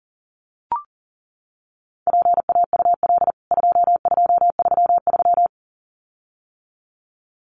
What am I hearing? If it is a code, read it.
Morse code, heard as a keyed tone, about 40 wpm, 700 Hz. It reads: PAUL 2233
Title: CWT Exchange 40wpm